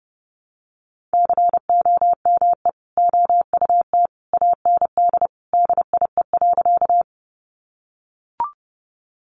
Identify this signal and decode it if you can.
Morse code, heard as a keyed tone, about 30 wpm, 700 Hz. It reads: COME OUT AND DIE.
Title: Sentences from Top 1300 Words 30wpm